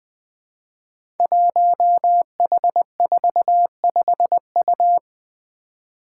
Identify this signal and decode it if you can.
Morse code, heard as a keyed tone, about 20 wpm, 700 Hz. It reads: 1H45U
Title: Five Challenging Letter-Number Combinations 20wpm